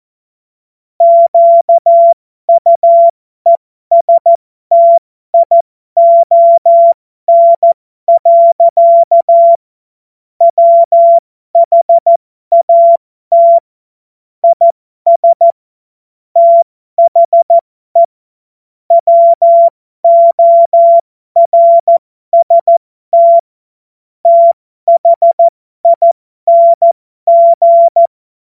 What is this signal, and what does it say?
Morse code, heard as a keyed tone, about 14 wpm, spaced as for 12 wpm, 680 Hz. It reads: QUESTION. WHAT IS THE WORST THING
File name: Morsekurs / Witze